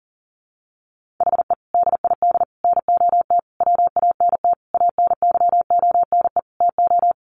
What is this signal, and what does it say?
Morse code, heard as a keyed tone, about 40 wpm, 700 Hz. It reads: HE DID NOT WANT ANYONE TO
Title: Sentences from Top 1100 Words 40wpm